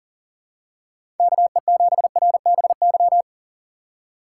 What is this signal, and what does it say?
Morse code, heard as a keyed tone, about 40 wpm, 700 Hz. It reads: KE7RBY
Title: USA Call Signs - 6 Characters - 3x Character Spacing - 40wpm